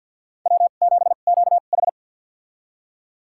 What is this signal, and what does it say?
Morse code, heard as a keyed tone, about 51 wpm, spaced as for 36 wpm, 700 Hz. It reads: W7QH
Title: USA Call Signs - 4 Characters - 2x Character Spacing - 50wpm